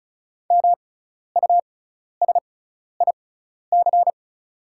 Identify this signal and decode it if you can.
Morse code, heard as a keyed tone, about 36 wpm, spaced as for 12 wpm, 690 Hz. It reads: MUSIC
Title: 6x Character Spacing - Top 500 Words - 35wpm